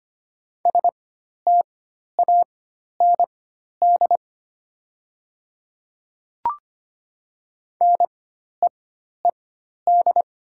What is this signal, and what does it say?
Morse code, heard as a keyed tone, about 25 wpm, spaced as for 12 wpm, 700 Hz. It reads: STAND NEED
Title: 4x Character Spacing - Top 300 Words - 25wpm